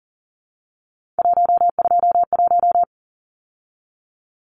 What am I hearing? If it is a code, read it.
Morse code, heard as a keyed tone, about 40 wpm, 700 Hz. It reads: 121
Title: Numbers - 3-Digit 40wpm